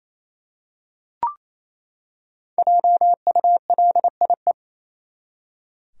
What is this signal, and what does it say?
Morse code, heard as a keyed tone, about 27 wpm, 700 Hz. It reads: JULIE
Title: USA Common Names - Max 5 Letters - 28wpm